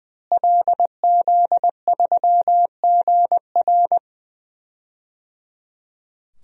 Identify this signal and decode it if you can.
Morse code, heard as a keyed tone, about 20 wpm, 700 Hz. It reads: LZ3GR
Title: DXCC-100 Call Signs - 5 Characters - Repeated 3x - 20wpm